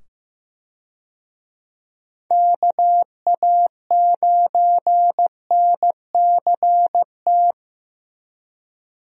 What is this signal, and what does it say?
Morse code, heard as a keyed tone, about 15 wpm, 700 Hz. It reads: KA9NCT
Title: USA Call Signs - 6 Characters - 15wpm